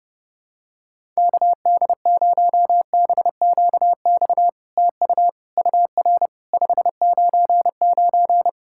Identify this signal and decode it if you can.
Morse code, heard as a keyed tone, about 32 wpm, 700 Hz. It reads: KD0BQX TU UR 599
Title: SOTA QSO Practice 30wpm